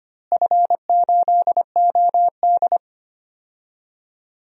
Morse code, heard as a keyed tone, about 25 wpm, 700 Hz. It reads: F8OD
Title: DXCC-100 Call Signs - 4 Characters - Repeated 3x - 25wpm